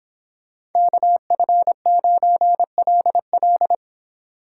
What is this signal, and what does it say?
Morse code, heard as a keyed tone, about 29 wpm, 700 Hz. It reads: KF9LL